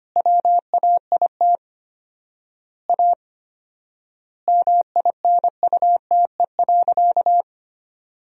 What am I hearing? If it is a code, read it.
Morse code, heard as a keyed tone, about 25 wpm, 700 Hz. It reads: WAIT A MINUTE.
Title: 4x Word-Spacing - Sentences from Top 500 Words - 25wpm